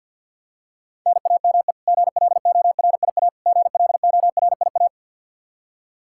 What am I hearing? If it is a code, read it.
Morse code, heard as a keyed tone, about 52 wpm, 700 Hz. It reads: NAME GLORIA GLORIA
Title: K1USN SST QSO Practice 50wpm